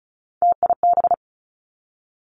Morse code, heard as a keyed tone, about 36 wpm, 700 Hz. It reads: TIB
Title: Three Challenging Letter-Number Combinations 35wpm